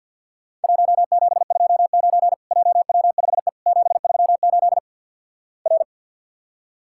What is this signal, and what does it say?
Morse code, heard as a keyed tone, about 50 wpm, 700 Hz. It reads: 1729 JW5E 738 R
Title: Worked All Europe DX Contest - QTC Exchange with Repeats - 50wpm